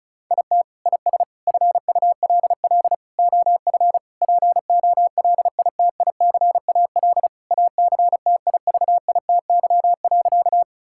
Morse code, heard as a keyed tone, about 36 wpm, 700 Hz. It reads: IT IS FULL OF POLITICAL ACTIVITY.
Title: Sentences from Top 1000 Words - Review Part 5 - 35wpm